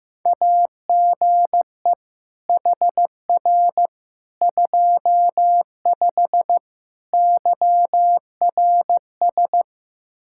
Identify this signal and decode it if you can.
Morse code, heard as a keyed tone, about 15 wpm, 700 Hz. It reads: AGE HR 25 YRS
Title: K1USN SST QSO Practice 15wpm